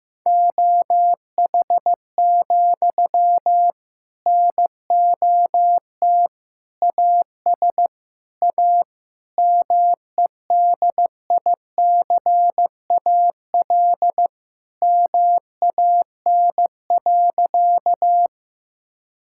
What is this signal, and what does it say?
Morse code, heard as a keyed tone, about 15 wpm, 700 Hz. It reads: OH, NOT AS A MEDICAL MAN.
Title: Sentences from Top 1000 Words - Review Part 5 - 15wpm